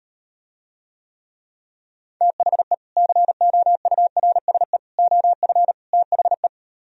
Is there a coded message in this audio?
Yes: Morse code from 2.2 to 6.5 s, heard as a keyed tone, about 38 wpm, 700 Hz. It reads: THE COURSE OF THE